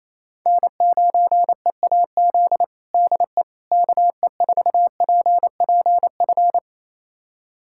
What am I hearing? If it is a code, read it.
Morse code, heard as a keyed tone, about 27 wpm, 700 Hz. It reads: N9EAZ DE KE4PPF
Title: USA Callsign DE Callsign 28wpm